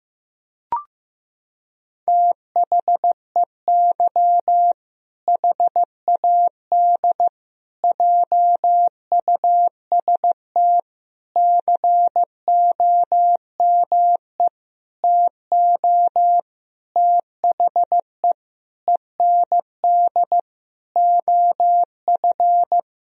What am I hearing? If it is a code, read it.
Morse code, heard as a keyed tone, about 15 wpm, 700 Hz. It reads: THEY HAD JUST COME TO THE END OF
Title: Sentences from Top 1000 Words - Review Part 2 - 15wpm